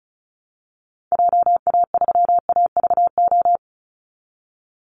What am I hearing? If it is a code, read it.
Morse code, heard as a keyed tone, about 34 wpm, 700 Hz. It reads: JA3AVO
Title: Call Signs - 35wpm